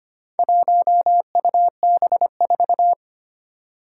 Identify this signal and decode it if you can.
Morse code, heard as a keyed tone, about 25 wpm, 700 Hz. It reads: 1UB4
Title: Four Challenging Letter-Number Combinations 25wpm